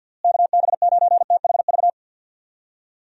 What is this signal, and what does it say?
Morse code, heard as a keyed tone, about 49 wpm, 700 Hz. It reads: KB9THV
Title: USA Call Signs - 6 Characters - Repeated 2x - 50wpm